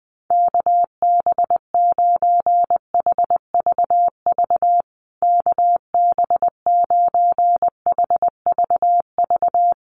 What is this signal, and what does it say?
Morse code, heard as a keyed tone, about 20 wpm, 700 Hz. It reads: KB9HVV KB9HVV
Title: USA Call Signs Repeated 3x 20wpm